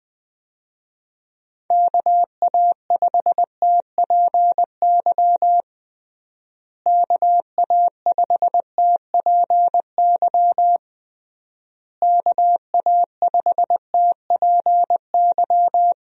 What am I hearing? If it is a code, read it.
Morse code, heard as a keyed tone, about 20 wpm, 700 Hz. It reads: KA5TPY KA5TPY KA5TPY
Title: USA Call Signs - 6 Characters - Repeated 3x - 20wpm